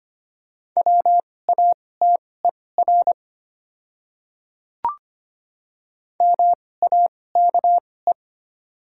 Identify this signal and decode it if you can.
Morse code, heard as a keyed tone, about 25 wpm, spaced as for 18 wpm, 700 Hz. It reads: WATER MAKE
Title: 2x Character Spacing - Top 100 Words - 25wpm